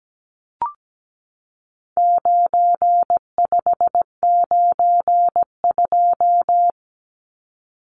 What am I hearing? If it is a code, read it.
Morse code, heard as a keyed tone, about 17 wpm, 700 Hz. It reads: 9592
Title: Numbers - 4-Digit 17wpm